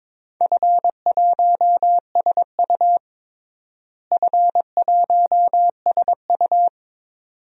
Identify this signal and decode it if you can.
Morse code, heard as a keyed tone, about 22 wpm, 700 Hz. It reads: F1SU F1SU
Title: DXCC-100 Call Signs - 4 Characters - Repeated 2x - 22wpm